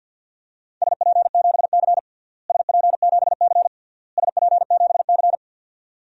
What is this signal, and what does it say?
Morse code, heard as a keyed tone, about 52 wpm, 700 Hz. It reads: SP7C SP7C SP7C
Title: DXCC-100 Call Signs - 4 Characters - Repeated 3x - 50wpm